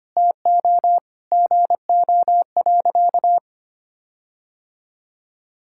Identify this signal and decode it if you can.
Morse code, heard as a keyed tone, about 25 wpm, 700 Hz. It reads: TO GO.
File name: practice / 25wpm / Sentences with 4-Word Phrases - 25wpm